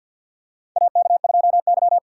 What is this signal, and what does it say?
Morse code, heard as a keyed tone, about 48 wpm, 700 Hz. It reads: AK2Y
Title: USA Call Signs - 4 Characters - 6x Character Spacing - 50wpm